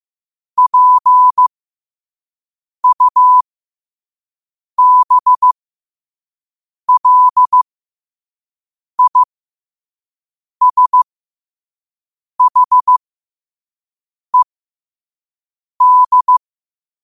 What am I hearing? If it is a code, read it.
Morse code, heard as a keyed tone, about 15 wpm, spaced as for 5 wpm, 1000 Hz. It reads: PUBLISHED
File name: qotdfiles / 5wpm